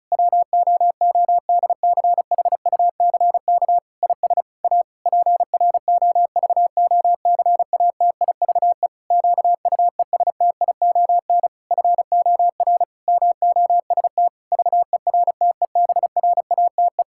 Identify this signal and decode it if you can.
Morse code, heard as a keyed tone, about 36 wpm, 700 Hz. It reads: WOODCHUCK IS A PROVOCATIVE QUESTION FOR MOST VERTEBRATE
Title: Pangram Challenge v2 - 35wpm